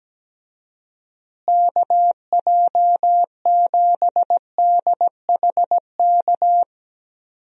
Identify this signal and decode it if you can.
Morse code, heard as a keyed tone, about 17 wpm, 700 Hz. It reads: KJ7DHK
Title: USA Call Signs - 6 Characters - Repeated 2x - 17wpm